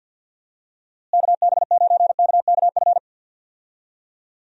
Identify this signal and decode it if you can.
Morse code, heard as a keyed tone, about 51 wpm, 700 Hz. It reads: KB9KKF
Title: USA Call Signs - 6 Characters - Repeated 2x - 50wpm